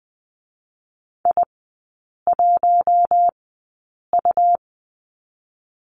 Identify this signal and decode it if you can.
Morse code, heard as a keyed tone, about 20 wpm, spaced as for 8 wpm, 700 Hz. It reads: I1U
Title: 121 Lesson 20e - Number 3 - Three Characters 20wpm